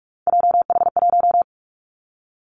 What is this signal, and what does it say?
Morse code, heard as a keyed tone, about 44 wpm, 700 Hz. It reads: JH1
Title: Callsign Prefixes for Top 25 DXCC - 45wpm